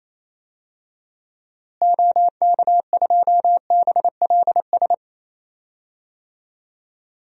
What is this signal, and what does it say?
Morse code, heard as a keyed tone, about 28 wpm, 700 Hz. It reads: OK2BLS
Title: DXCC-100 Call Signs - 6 Characters - Repeated 2x - 28wpm